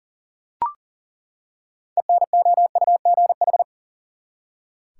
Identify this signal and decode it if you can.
Morse code, heard as a keyed tone, about 40 wpm, 700 Hz. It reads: ENOUGH
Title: Six Letter Words - 40wpm